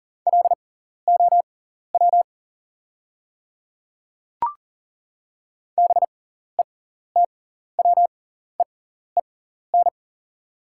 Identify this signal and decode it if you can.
Morse code, heard as a keyed tone, about 40 wpm, spaced as for 14 wpm, 690 Hz. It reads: LOW BETWEEN
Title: 6x Character Spacing - Top 300 Words - 40wpm